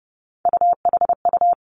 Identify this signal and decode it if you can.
Morse code, heard as a keyed tone, about 30 wpm, 700 Hz. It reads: UHU
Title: Three Challenging Letter-Number Combinations 30wpm